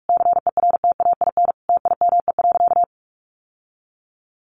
Morse code, heard as a keyed tone, about 44 wpm, 700 Hz. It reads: CERTAIN TIME.
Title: Sentences from Top 1000 Words 45wpm